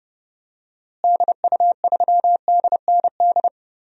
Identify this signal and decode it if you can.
Morse code, heard as a keyed tone, about 30 wpm, 700 Hz. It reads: DU3DND
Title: DXCC-100 Call Signs - 6 Characters - Repeated 2x - 30wpm